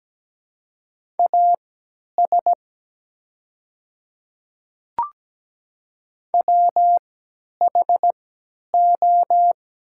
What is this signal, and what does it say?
Morse code, heard as a keyed tone, about 17 wpm, spaced as for 10 wpm, 700 Hz. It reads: AS WHO
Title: 3x Character Spacing - Top 200 Words - 17wpm